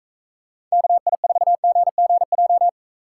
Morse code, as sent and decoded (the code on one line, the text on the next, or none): -.- .. ....- --. --. .---
KI4GGJ